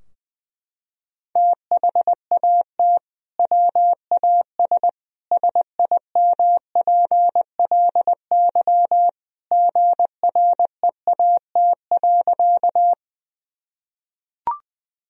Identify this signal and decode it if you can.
Morse code, heard as a keyed tone, about 20 wpm, 700 Hz. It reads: THAT WAS SIMPLY GREAT.
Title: Sentences from Top 1000 Words - Review Part 4 - 20wpm